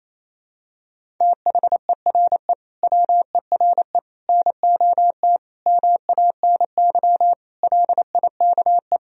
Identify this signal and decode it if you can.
Morse code, heard as a keyed tone, about 28 wpm, 700 Hz. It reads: THERE WERE NOT MANY LIKE
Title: Sentences from Top 1000 Words - Review Part 1 - 28wpm